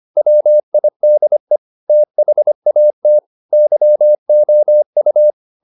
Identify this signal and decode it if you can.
Morse code, heard as a keyed tone, about 25 wpm, 600 Hz. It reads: WIDE THAT YOU